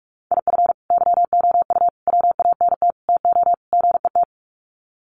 Morse code, heard as a keyed tone, about 44 wpm, 700 Hz. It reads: IF YOU WANT TO GET
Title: Sets of 5 Words Encore 45wpm